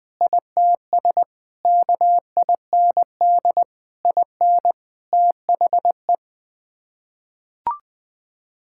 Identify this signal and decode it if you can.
Morse code, heard as a keyed tone, about 20 wpm, 700 Hz. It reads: ITS KIND IN THE
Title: Sets of 8 Words - 20wpm